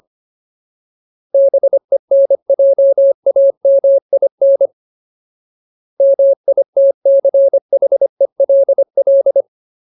Morse code, heard as a keyed tone, about 25 wpm, 550 Hz. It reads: BENJAMIN MITCHELL
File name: Ultimate Word List / 25WPM Ultimate